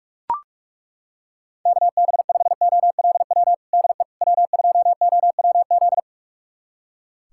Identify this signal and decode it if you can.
Morse code, heard as a keyed tone, about 42 wpm, 700 Hz. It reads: KB5OLW DE W2OWZ
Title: USA Callsign DE Callsign 45wpm